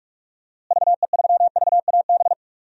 Morse code, heard as a keyed tone, about 46 wpm, 700 Hz. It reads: VE3VAB
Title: DXCC-100 Call Signs - 6 Characters - Repeated 3x - 45wpm